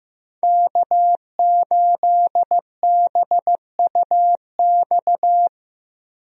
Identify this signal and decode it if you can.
Morse code, heard as a keyed tone, about 15 wpm, 700 Hz. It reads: K8BUX
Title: USA Call Signs - 5 Characters - Repeated 3x - 15wpm